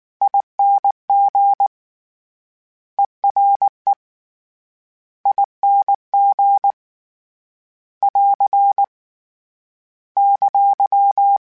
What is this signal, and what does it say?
Morse code, heard as a keyed tone, about 19 wpm, 800 Hz. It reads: ING ERE ING <AR> !
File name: Special INGs / Special INGs-mp3s / Ere ing @18